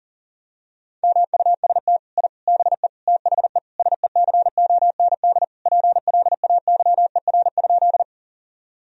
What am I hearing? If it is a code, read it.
Morse code, heard as a keyed tone, about 40 wpm, 700 Hz. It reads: MUST I BE THE SECOND PLAYER?
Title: Sentences from Top 1000 Words - Review Part 4 - 40wpm